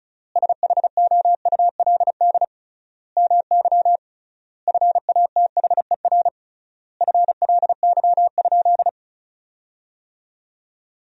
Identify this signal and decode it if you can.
Morse code, heard as a keyed tone, about 36 wpm, 700 Hz. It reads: SHOULD MY FATHER FLY?
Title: 3x Word-Spacing - Sentences from Top 500 Words - 35wpm